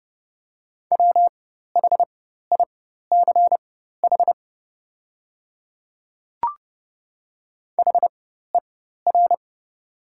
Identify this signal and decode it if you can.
Morse code, heard as a keyed tone, about 31 wpm, spaced as for 14 wpm, 700 Hz. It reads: WHICH HER
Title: 4x Character Spacing - Top 200 Words - 30wpm